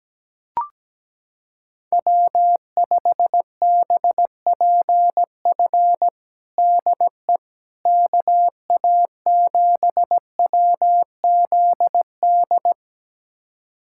Morse code, heard as a keyed tone, about 17 wpm, 700 Hz. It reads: W5BPF DE KA7WZD
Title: USA Callsign DE Callsign 17wpm